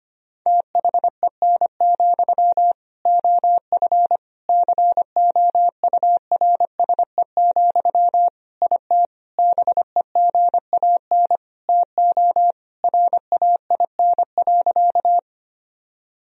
Morse code, heard as a keyed tone, about 25 wpm, 700 Hz. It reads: THEN, OF COURSE, IT BEGAN TO RAIN.
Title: Sentences from Top 1000 Words - Review Part 1 - 25wpm